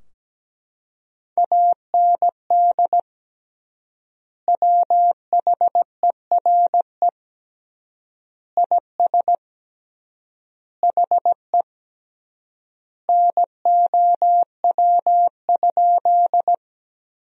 Morse code, heard as a keyed tone, about 17 wpm, 700 Hz. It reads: AND WHERE IS HE NOW?
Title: 3x Word-Spacing - Sentences from Top 200 Words - 17wpm